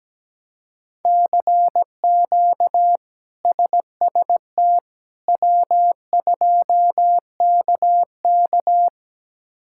Morse code, heard as a keyed tone, about 17 wpm, 700 Hz. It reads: CQ SST W2KK
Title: K1USN SST QSO Practice - 17wpm